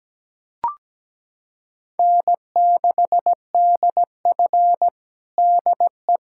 Morse code, heard as a keyed tone, about 17 wpm, 700 Hz. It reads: N6DF DE
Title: USA Callsign DE Callsign - 4 Characters - 17wpm